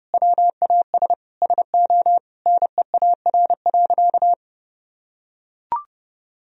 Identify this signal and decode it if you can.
Morse code, heard as a keyed tone, about 30 wpm, 700 Hz. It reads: WAS SO NEAR.
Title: Sentences from Top 300 Words and Abbreviations 30wpm